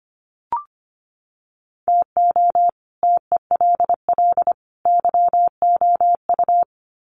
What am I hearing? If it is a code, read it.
Morse code, heard as a keyed tone, about 25 wpm, 700 Hz. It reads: TO TELL YOU
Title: Sets of 3 Words 25wpm